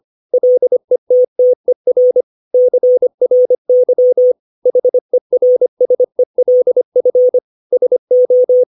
Morse code, heard as a keyed tone, about 25 wpm, 500 Hz. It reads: LETTER CRY HERSELF SO
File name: 632 words / 632 Words 25WPM